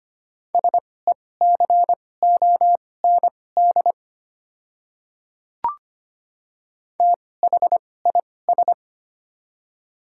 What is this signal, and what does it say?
Morse code, heard as a keyed tone, about 25 wpm, spaced as for 18 wpm, 690 Hz. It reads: SECOND THIS
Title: 2x Character Spacing - Top 300 Words - 25wpm